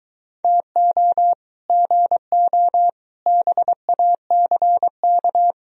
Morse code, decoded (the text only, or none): TO GO BACK